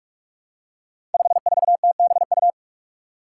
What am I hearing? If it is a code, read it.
Morse code, heard as a keyed tone, about 44 wpm, 700 Hz. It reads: 54TBU
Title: Five Challenging Letter-Number Combinations 45wpm